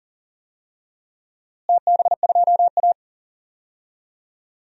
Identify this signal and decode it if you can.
Morse code, heard as a keyed tone, about 39 wpm, 700 Hz. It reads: TB2A
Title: DXCC-100 Call Signs - 4 Characters - Repeated 3x - 40wpm